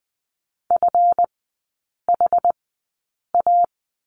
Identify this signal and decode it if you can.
Morse code, heard as a keyed tone, about 20 wpm, spaced as for 8 wpm, 700 Hz. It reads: FHA
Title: 152 Lesson 25e - Letter p - Three Characters 20wpm